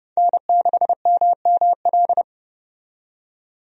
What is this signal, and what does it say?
Morse code, heard as a keyed tone, about 29 wpm, 700 Hz. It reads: N6MML
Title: USA Call Signs - 5 Characters - 30wpm